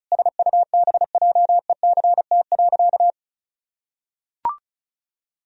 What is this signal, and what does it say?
Morse code, heard as a keyed tone, about 34 wpm, 700 Hz. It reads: SUBJECT.
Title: Sentences from Top 1000 Words - Review Part 1 - 35wpm